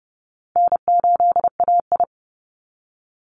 Morse code, heard as a keyed tone, about 28 wpm, 700 Hz. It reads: N8AI
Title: Easy Call Signs Repeated 3x - 30wpm